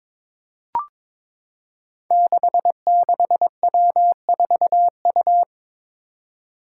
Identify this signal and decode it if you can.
Morse code, heard as a keyed tone, about 21 wpm, 700 Hz. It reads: 66W4U
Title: Five Challenging Letter-Number Combinations 22wpm